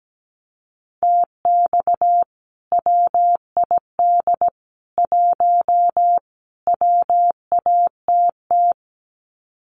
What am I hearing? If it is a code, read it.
Morse code, heard as a keyed tone, about 17 wpm, 700 Hz. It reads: TX WID 1 WATT
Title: Common QSO Phrases 17wpm